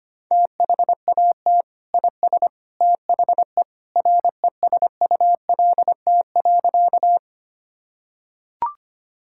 Morse code, heard as a keyed tone, about 25 wpm, 700 Hz. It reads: THAT IS THE RESULT.
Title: Sentences from Top 1000 Words - Review Part 5 - 25wpm